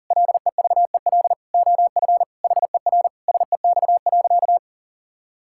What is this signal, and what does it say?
Morse code, heard as a keyed tone, about 40 wpm, 700 Hz. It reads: LEVEL OF HER SEX.
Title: Sentences from Top 1600 Words 40wpm